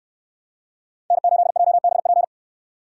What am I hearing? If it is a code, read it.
Morse code, heard as a keyed tone, about 67 wpm, 700 Hz. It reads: N8JBP